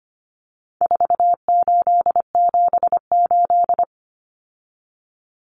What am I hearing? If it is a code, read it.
Morse code, heard as a keyed tone, about 25 wpm, 700 Hz. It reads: 4878
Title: Common Cut Numbers - 4-Digit 25wpm